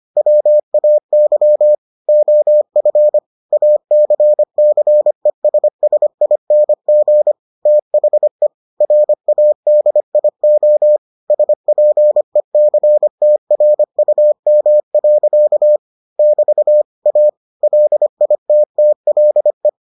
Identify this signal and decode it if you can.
Morse code, heard as a keyed tone, about 25 wpm, 600 Hz. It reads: WAY OF ACCESSING THE RADIO SPECTRUM. <BT> A LITTLE